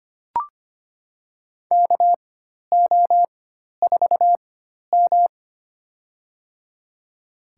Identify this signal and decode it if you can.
Morse code, heard as a keyed tone, about 25 wpm, spaced as for 12 wpm, 700 Hz. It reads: KO4M
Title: USA Call Signs - 4x Character Spacing - 25wpm